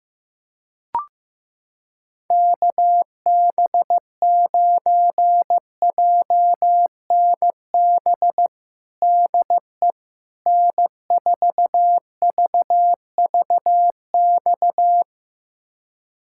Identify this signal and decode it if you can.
Morse code, heard as a keyed tone, about 15 wpm, 700 Hz. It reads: KB9JNB DE N4VVX
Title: USA Callsign DE Callsign 15wpm